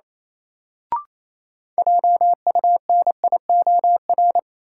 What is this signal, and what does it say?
Morse code, heard as a keyed tone, about 28 wpm, 700 Hz. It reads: JUNIOR
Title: Sentences from Top 1000 Words - Review Part 7 - New Word Drill - 28wpm